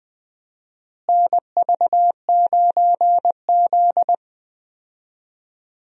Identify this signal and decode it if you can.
Morse code, heard as a keyed tone, about 20 wpm, 700 Hz. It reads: NV9Z
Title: USA Call Signs - 4 Characters - Repeated 2x - 20wpm